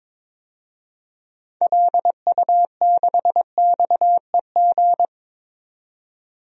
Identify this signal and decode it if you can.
Morse code, heard as a keyed tone, about 21 wpm, 700 Hz. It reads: LU6XEG
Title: DXCC-100 Call Signs - 6 Characters - 22wpm